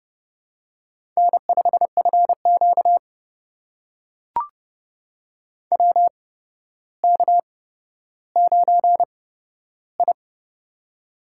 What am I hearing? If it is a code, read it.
Morse code, heard as a keyed tone, about 29 wpm, spaced as for 13 wpm, 700 Hz. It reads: N5FQ WK9I
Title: USA Call Signs - 4 Characters - 8x Character Spacing - 30wpm